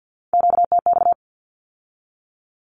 Morse code, heard as a keyed tone, about 50 wpm, 700 Hz. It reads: QTX
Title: Q-Codes with Spoken Meaning - 50wpm